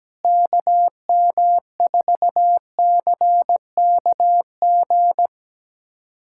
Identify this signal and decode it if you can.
Morse code, heard as a keyed tone, about 17 wpm, 700 Hz. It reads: KM4CKG
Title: USA Callsign DE Callsign 17wpm